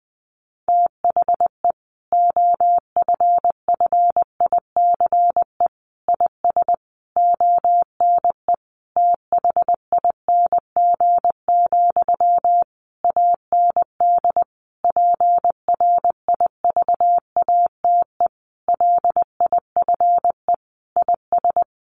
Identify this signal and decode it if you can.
Morse code, heard as a keyed tone, about 20 wpm, 700 Hz. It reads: THE OFFICE IS ONE THING, AND PRIVATE LIFE IS
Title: Sentences from Top 1000 Words 20wpm